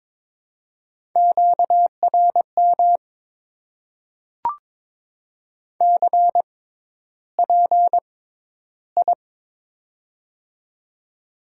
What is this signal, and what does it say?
Morse code, heard as a keyed tone, about 22 wpm, spaced as for 11 wpm, 700 Hz. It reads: QRM CPI
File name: Top 90 QSO Elements - 6x Character Spacing - 22wpm